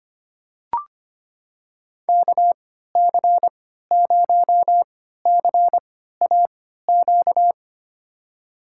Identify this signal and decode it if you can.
Morse code, heard as a keyed tone, about 25 wpm, spaced as for 14 wpm, 700 Hz. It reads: KC0CAQ
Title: USA Call Signs - 6 Characters - 3x Character Spacing - 25wpm